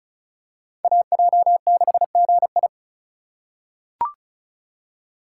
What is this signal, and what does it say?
Morse code, heard as a keyed tone, about 34 wpm, 700 Hz. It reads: AJ6GI
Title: USA Call Signs - 5 Characters - 4x Character Spacing - 35wpm